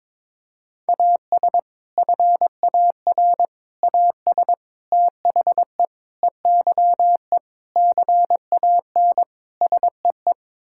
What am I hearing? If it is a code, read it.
Morse code, heard as a keyed tone, about 22 wpm, 700 Hz. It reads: AS FAR AS THE EYE CAN SEE
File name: practice / 22wpm / Sets of 7 Words - 22wpm